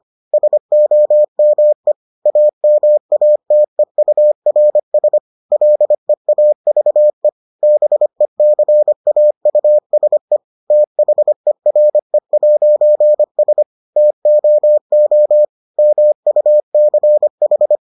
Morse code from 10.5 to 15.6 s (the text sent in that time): THERE'S TOO